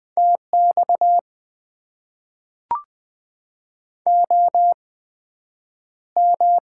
Morse code, heard as a keyed tone, about 20 wpm, spaced as for 9 wpm, 700 Hz. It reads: TX OM
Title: Top 90 QSO Elements - 8x Character Spacing - 20wpm